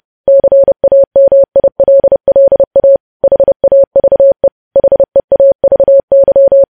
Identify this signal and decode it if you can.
Morse code, heard as a keyed tone, about 30 wpm, 550 Hz. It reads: CAMILLA HAVE HEAVY
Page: News Headlines in Morse Code at 30 WPM